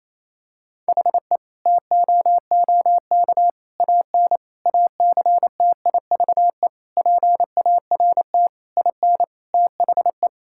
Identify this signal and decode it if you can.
Morse code, heard as a keyed tone, about 28 wpm, 700 Hz. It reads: HE TOOK AN ACTIVE PART IN THE
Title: Sentences from Top 1000 Words - Review Part 6 - 28wpm